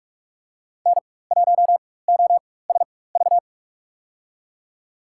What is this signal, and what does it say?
Morse code, heard as a keyed tone, about 45 wpm, spaced as for 21 wpm, 700 Hz. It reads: N1OSV
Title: USA Call Signs - 4x Character Spacing - 45wpm